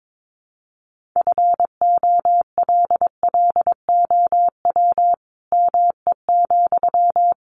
Morse code, heard as a keyed tone, about 22 wpm, 700 Hz. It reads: FOLLOW ME,
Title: Sentences from Top 200 Words 22wpm